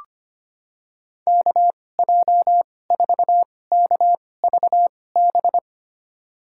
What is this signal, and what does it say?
Morse code, heard as a keyed tone, about 25 wpm, spaced as for 18 wpm, 700 Hz. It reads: KJ4KVB
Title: USA Call Signs - 2x Character Spacing - 25wpm